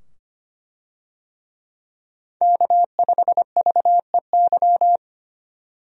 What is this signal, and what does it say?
Morse code, heard as a keyed tone, about 25 wpm, 700 Hz. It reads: K5VEY
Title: USA Call Signs - 5 Characters - 25wpm